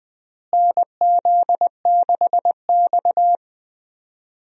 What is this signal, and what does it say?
Morse code, heard as a keyed tone, about 20 wpm, 700 Hz. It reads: NZ6X
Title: USA Call Signs - 4 Characters - Repeated 2x - 20wpm